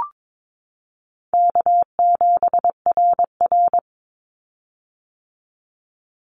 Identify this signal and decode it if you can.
Morse code, heard as a keyed tone, about 22 wpm, 700 Hz. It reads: K7RR
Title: Easy Call Signs - 22wpm